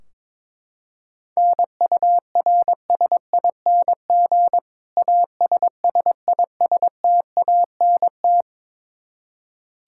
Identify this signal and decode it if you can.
Morse code, heard as a keyed tone, about 21 wpm, 700 Hz. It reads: NURSING ASSISTANT
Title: Practice QSO - Exchange 3 - More Info - 22wpm